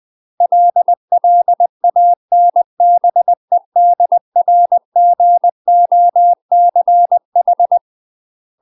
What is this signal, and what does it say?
Morse code, heard as a keyed tone, about 20 wpm, 700 Hz. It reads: LLANBEDRGOCH
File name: UK MASTER QSOs TRAINING / 20WPM / UK MASTER QSOs 0031